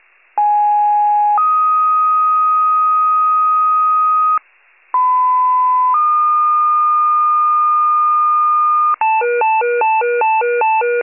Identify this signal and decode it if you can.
Two-tone page, heard as a one-second tone, then a three-second tone, about 830 then 1250 Hz, about 1010 then 1220 Hz. It is heard from 0.4 to 8.9 s.
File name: pager/94 RESCUE OFC